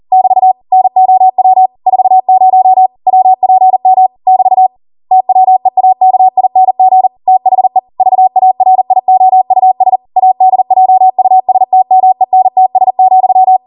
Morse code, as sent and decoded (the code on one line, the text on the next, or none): -...- / -. --- .-- / ....- ----- / .-- .--. -- / -...- / - .-- . .- -.- .. -. --. / - .... / ...- .- .-. .. --- ..- ... / .- -.. .--- ..- ... - -- . -. - ... --..--
<BT> NOW 40 WPM <BT> TWEAKING THE VARIOUS ADJUSTMENTS,